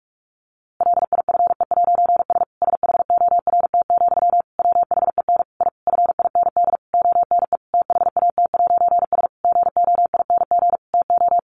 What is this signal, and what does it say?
Morse code, heard as a keyed tone, about 44 wpm, 700 Hz. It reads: LIFE'S SHORT, WHEN I FIND ONE THAT'S GOING TO
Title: A Sound Of Thunder 45wpm